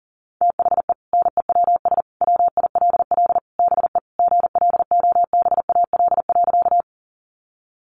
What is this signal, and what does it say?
Morse code, heard as a keyed tone, about 40 wpm, 700 Hz. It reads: THE NEWS WILL BE GLOBAL.
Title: Sentences from Top 1200 Words 40wpm